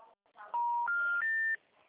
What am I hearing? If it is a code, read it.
Special information tone.